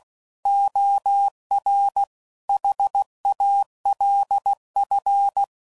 Morse code, heard as a keyed tone, about 16 wpm, 800 Hz. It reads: OR HALF